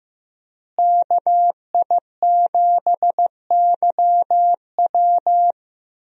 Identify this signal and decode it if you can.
Morse code, heard as a keyed tone, about 15 wpm, 700 Hz. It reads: KI7YW
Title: USA Call Signs - 5 Characters - Repeated 3x - 15wpm